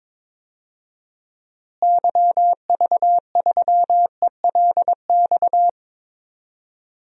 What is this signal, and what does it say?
Morse code, heard as a keyed tone, about 22 wpm, 700 Hz. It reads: YV3ELX
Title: DXCC-100 Call Signs - 6 Characters - Repeated 2x - 22wpm